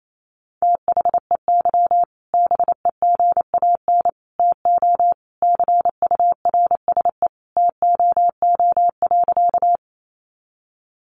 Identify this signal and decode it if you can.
Morse code, heard as a keyed tone, about 28 wpm, 700 Hz. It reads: THEY BEGAN TO CURSE TOO.
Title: A Sound Of Thunder 28wpm